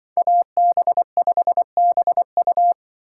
Morse code, heard as a keyed tone, about 25 wpm, 700 Hz. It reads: AB5BU